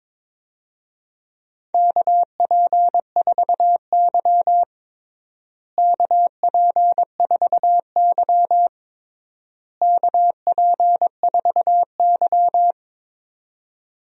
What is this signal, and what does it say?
Morse code, heard as a keyed tone, about 22 wpm, 700 Hz. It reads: KP4Y KP4Y KP4Y
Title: DXCC-100 Call Signs - 4 Characters - Repeated 3x - 22wpm